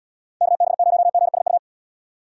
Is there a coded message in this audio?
Yes: Morse code from 0.4 to 1.6 s, heard as a keyed tone, about 72 wpm, 700 Hz. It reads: KB0GSU